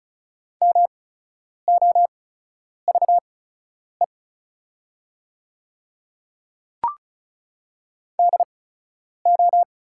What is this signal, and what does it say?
Morse code, heard as a keyed tone, about 36 wpm, spaced as for 10 wpm, 700 Hz. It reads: MOVE DO
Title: 8x Character Spacing - Top 200 Words - 35wpm